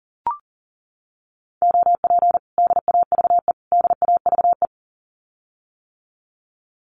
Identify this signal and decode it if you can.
Morse code, heard as a keyed tone, about 40 wpm, 700 Hz. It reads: OP DAVE DAVE
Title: Common QSO Phrases Encore 40wpm